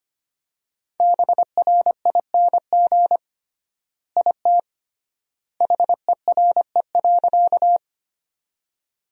Morse code, heard as a keyed tone, about 25 wpm, 700 Hz. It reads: BRING IT HERE.
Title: 3x Word-Spacing - Sentences from Top 500 Words - 25wpm